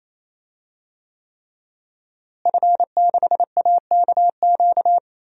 Morse code, heard as a keyed tone, about 28 wpm, 700 Hz. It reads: F6AKQ
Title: DXCC-100 Call Signs - 5 Characters - Repeated 3x - 28wpm